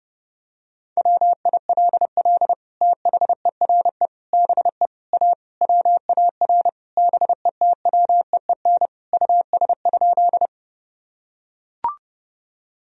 Morse code, heard as a keyed tone, about 32 wpm, 700 Hz. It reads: WILL THERE BE A WAR BETWEEN US?
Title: Sentences from Top 1000 Words - Review Part 2 - 30wpm